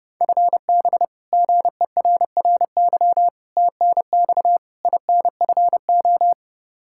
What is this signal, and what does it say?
Morse code, heard as a keyed tone, about 30 wpm, 700 Hz. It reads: FB GERRY TNX INFO
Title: Practice QSO - Exchange 3 - More Info - 30wpm